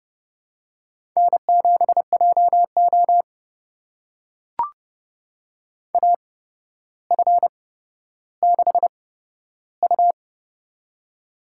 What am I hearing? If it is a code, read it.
Morse code, heard as a keyed tone, about 30 wpm, spaced as for 13 wpm, 700 Hz. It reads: N7JO AF6U
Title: USA Call Signs - 4 Characters - 8x Character Spacing - 30wpm